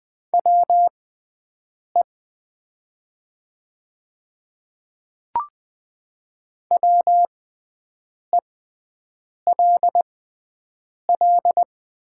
Morse code, heard as a keyed tone, about 20 wpm, spaced as for 7 wpm, 700 Hz. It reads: WE WELL
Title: 6x Character Spacing - Top 300 Words - 20wpm